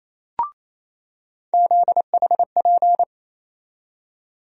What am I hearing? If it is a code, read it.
Morse code, heard as a keyed tone, about 28 wpm, 700 Hz. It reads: ZHP
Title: USA Call Sign Postfixes - 3 Letters 28wpm